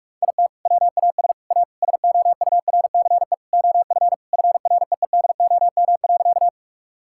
Morse code, heard as a keyed tone, about 46 wpm, 700 Hz. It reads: IT WAS A SOURCE OF FREEDOM.
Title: Sentences from Top 1000 Words - Review Part 5 - 45wpm